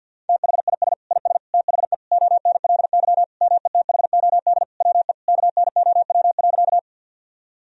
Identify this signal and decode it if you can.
Morse code, heard as a keyed tone, about 51 wpm, 700 Hz. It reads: THIS IS THE ONLY METHOD WE KNOW.
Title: Sentences from Top 1000 Words - Review Part 3 - 50wpm